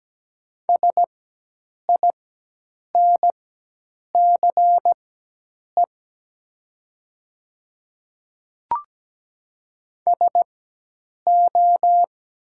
Morse code, heard as a keyed tone, about 17 wpm, spaced as for 8 wpm, 700 Hz. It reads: SINCE SO
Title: 4x Character Spacing - Top 300 Words - 17wpm